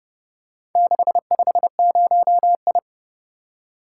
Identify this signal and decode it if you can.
Morse code, heard as a keyed tone, about 29 wpm, 700 Hz. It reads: 650I